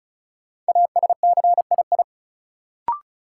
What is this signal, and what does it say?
Morse code, heard as a keyed tone, about 36 wpm, 700 Hz. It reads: ASCII
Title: Sets of 5 Words - New Word Drill - 35wpm